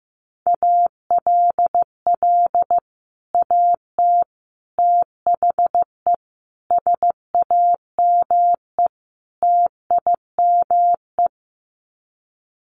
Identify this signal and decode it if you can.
Morse code, heard as a keyed tone, about 15 wpm, 700 Hz. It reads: ALL AT THE SAME TIME
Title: Sets of 5 Words Encore2 15wpm